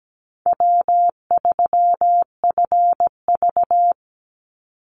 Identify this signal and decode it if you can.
Morse code, heard as a keyed tone, about 17 wpm, 700 Hz. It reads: W3FV
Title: Easy Call Signs Repeated 3x - 17wpm